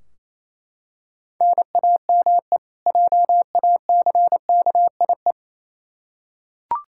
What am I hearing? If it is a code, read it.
Morse code, heard as a keyed tone, about 28 wpm, 700 Hz. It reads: NAME JACKIE
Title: K1USN SST QSO Practice 28wpm